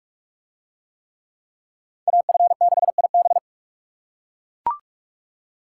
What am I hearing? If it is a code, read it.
Morse code, heard as a keyed tone, about 39 wpm, 700 Hz. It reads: AF6IB